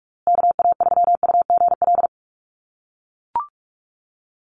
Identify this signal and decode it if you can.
Morse code, heard as a keyed tone, about 44 wpm, 700 Hz. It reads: KA3UGL
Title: USA Call Signs Repeated 2x 45wpm